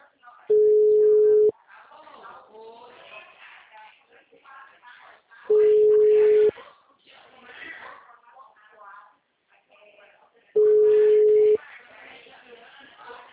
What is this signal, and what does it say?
Ringback tone (Europe, 425 Hz), with speech.